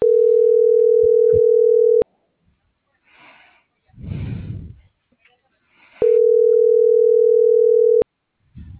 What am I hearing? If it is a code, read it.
Ringback tone (North America), 440 + 480 Hz.